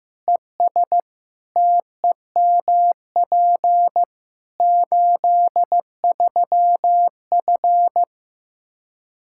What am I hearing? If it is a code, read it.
Morse code, heard as a keyed tone, about 15 wpm, 700 Hz. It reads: ES TEMP 83F
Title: K1USN SST QSO Practice 15wpm